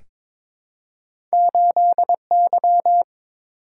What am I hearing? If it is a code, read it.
Morse code, heard as a keyed tone, about 22 wpm, 700 Hz. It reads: 8Y
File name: practice / 22wpm / Two Letter-Number Combinations - 22wpm